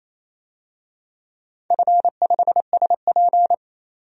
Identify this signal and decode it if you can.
Morse code, heard as a keyed tone, about 28 wpm, 700 Hz. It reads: F5SP
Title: DXCC-100 Call Signs - 4 Characters - 28wpm